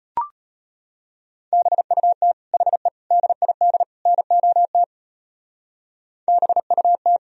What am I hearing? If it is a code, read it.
Morse code, heard as a keyed tone, about 31 wpm, 700 Hz. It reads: BUT HE DID NOT BUT
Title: Speed-Racing - Sets of 4 Words 25wpm